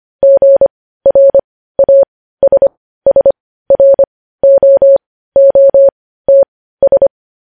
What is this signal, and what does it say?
Morse code, heard as a keyed tone, about 25 wpm, spaced as for 15 wpm, 550 Hz. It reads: GRASSROOTS
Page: News Headlines in Morse Code at 15 WPM